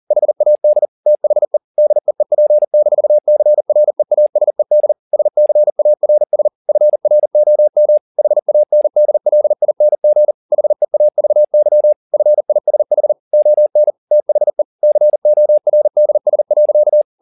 Morse code, heard as a keyed tone, about 40 wpm, 600 Hz. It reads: HAD THE DEEP-CREASED SCARS FROM HANDLING HEAVY FISH ON THE CORDS.
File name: the old man and the sea 1 40wpm